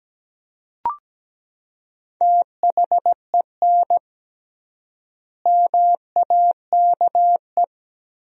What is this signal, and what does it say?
Morse code, heard as a keyed tone, about 17 wpm, 700 Hz. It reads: THEN MAKE
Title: 3x Word-Spacing - Sentences from Top 100 Words - 17wpm